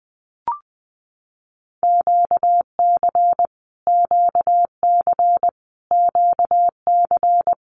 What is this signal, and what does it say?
Morse code, heard as a keyed tone, about 20 wpm, 700 Hz. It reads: QC QC QC
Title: Canadian Provinces and Territories - Repeated 3x - 20wpm